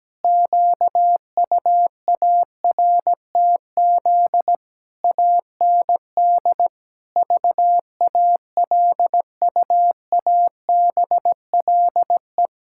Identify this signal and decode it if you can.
Morse code, heard as a keyed tone, about 17 wpm, 700 Hz. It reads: QUARTZ AND VALUABLE
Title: Pangram Challenge v2 - 17wpm